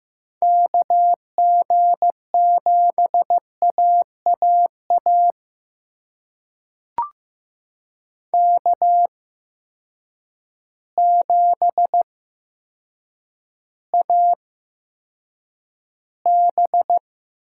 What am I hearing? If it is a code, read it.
Morse code, heard as a keyed tone, about 15 wpm, spaced as for 8 wpm, 700 Hz. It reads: KG7AAA K7AB
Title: USA Call Signs - 8x Character Spacing - 15wpm